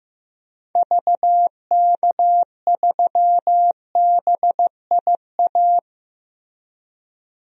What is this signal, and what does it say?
Morse code, heard as a keyed tone, about 16 wpm, 700 Hz. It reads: VK3BIA
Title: DXCC-100 Call Signs - 6 Characters - Repeated 2x - 15wpm